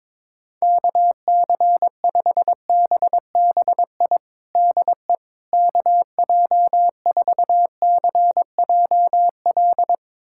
Morse code, heard as a keyed tone, about 22 wpm, 700 Hz. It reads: KC5BBI DE KJ4CJL
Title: USA Callsign DE Callsign - 6 Characters - 22wpm